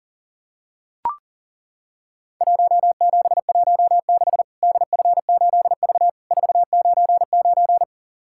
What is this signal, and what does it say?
Morse code, heard as a keyed tone, about 40 wpm, 700 Hz. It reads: 1716 DF8V 499
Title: Worked All Europe DX Contest - QTC Exchange - 40wpm